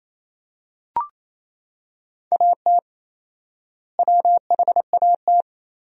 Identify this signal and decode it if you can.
Morse code, heard as a keyed tone, about 27 wpm, 700 Hz. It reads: AT WHAT
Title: 4x Word-Spacing - Sentences from Top 100 Words - 28wpm